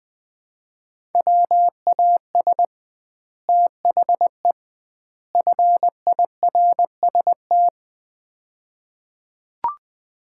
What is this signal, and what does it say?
Morse code, heard as a keyed tone, about 20 wpm, 700 Hz. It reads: WAS THE FIRST
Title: 2x Word-Spacing - Sets of 3 Words - 20wpm